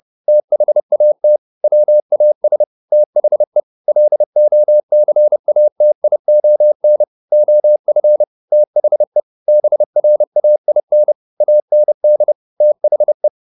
Morse code, heard as a keyed tone, about 30 wpm, 600 Hz. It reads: THAT WAS THE LOCATION OF THE BRAIN AND THE